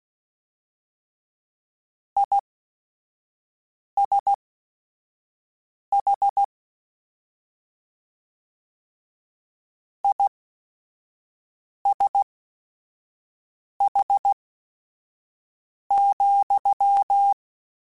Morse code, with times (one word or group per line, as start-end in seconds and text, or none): 2.2-6.4 ISH
10.0-17.3 ISH,